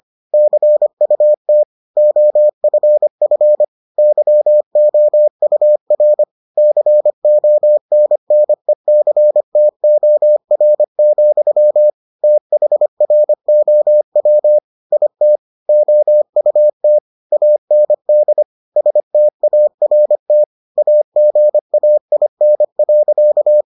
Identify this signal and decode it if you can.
Morse code, heard as a keyed tone, about 25 wpm, 600 Hz. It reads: CUT OFF YOUR CONNECTOR, THROW IT OUT AND START AGAIN.